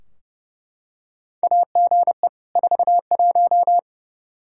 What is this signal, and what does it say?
Morse code, heard as a keyed tone, about 30 wpm, 700 Hz. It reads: AGE 41
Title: Practice QSO - Exchange 3 - More Info - 30wpm